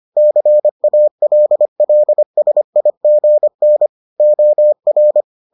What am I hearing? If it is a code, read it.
Morse code, heard as a keyed tone, about 25 wpm, 600 Hz. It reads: CALLSIGN OR